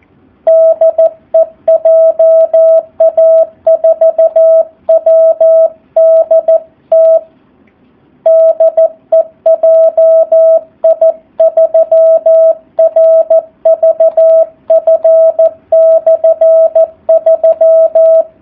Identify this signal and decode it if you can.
Morse code, heard as a keyed tone, about 14 wpm, 650 Hz. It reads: DE JA4WDT DE JI3RVF/3
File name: jh8cbh / 2019cw / SP04- w